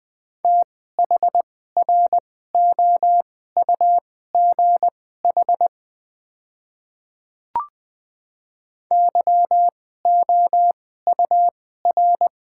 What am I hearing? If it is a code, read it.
Morse code, heard as a keyed tone, about 20 wpm, spaced as for 14 wpm, 700 Hz. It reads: THROUGH YOUR
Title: 2x Character Spacing - Top 200 Words - 20wpm